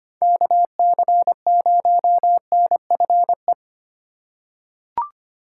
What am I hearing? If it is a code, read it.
Morse code, heard as a keyed tone, about 25 wpm, 700 Hz. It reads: KC0NFE
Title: USA Call Signs - 6 Characters - 3x Character Spacing - 25wpm